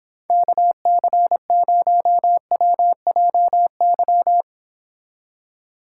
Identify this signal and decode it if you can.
Morse code, heard as a keyed tone, about 26 wpm, 700 Hz. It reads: KC0WJY